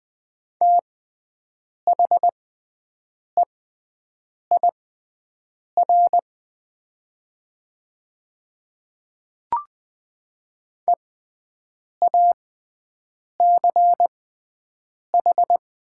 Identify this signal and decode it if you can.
Morse code, heard as a keyed tone, about 20 wpm, spaced as for 7 wpm, 700 Hz. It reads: THEIR EACH
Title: 6x Character Spacing - Top 100 Words - 20wpm